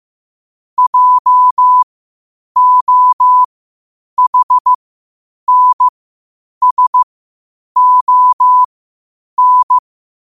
Morse code, heard as a keyed tone, about 15 wpm, spaced as for 8 wpm, 1000 Hz. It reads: JOHNSON
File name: qotdfiles / 8wpm